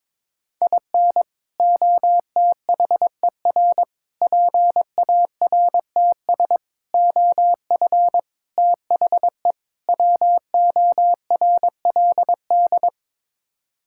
Morse code, heard as a keyed tone, about 22 wpm, 700 Hz. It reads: IN OTHER PARTS OF THE WORLD
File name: practice / 22wpm / Sets of 6 Words - 22wpm